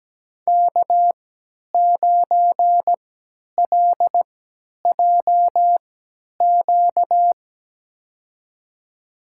Morse code, heard as a keyed tone, about 17 wpm, spaced as for 10 wpm, 700 Hz. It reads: K9LJQ
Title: USA Call Signs - 5 Characters - 3x Character Spacing - 17wpm